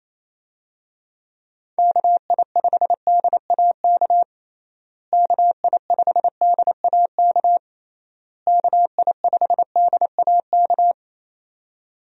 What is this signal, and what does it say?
Morse code, heard as a keyed tone, about 28 wpm, 700 Hz. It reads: KI5DAK KI5DAK KI5DAK
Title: USA Call Signs - 6 Characters - Repeated 3x - 28wpm